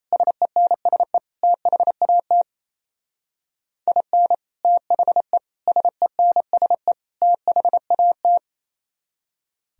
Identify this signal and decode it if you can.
Morse code, heard as a keyed tone, about 34 wpm, 700 Hz. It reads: SENSE THAT IN THE SENSE THAT
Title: Speed-Racing - Sets of 4 Words 28wpm